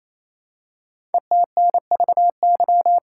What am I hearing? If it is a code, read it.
Morse code, heard as a keyed tone, about 29 wpm, 700 Hz. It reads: ETNVY
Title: Code Groups - 5 Letters - 28wpm